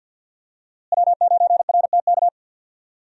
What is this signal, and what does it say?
Morse code, heard as a keyed tone, about 51 wpm, 700 Hz. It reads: W9RTK
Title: USA Call Signs - 5 Characters - Repeated 2x - 50wpm